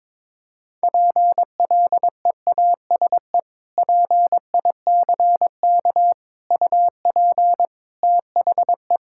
Morse code, heard as a keyed tone, about 22 wpm, 700 Hz. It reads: PLEASE PICK UP THE
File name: practice / 22wpm / Sentences with 2-Word Phrases - 22wpm